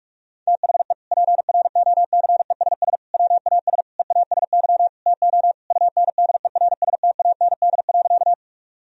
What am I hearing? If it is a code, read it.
Morse code, heard as a keyed tone, about 46 wpm, 700 Hz. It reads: THE PROCESS WAS EASY TO UNDERSTAND.
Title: Sentences from Top 1000 Words - Review Part 3 - 45wpm